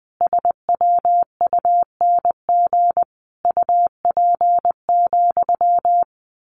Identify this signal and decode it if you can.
Morse code, heard as a keyed tone, about 20 wpm, 700 Hz. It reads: SWUNG UP,
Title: A Sound Of Thunder 20wpm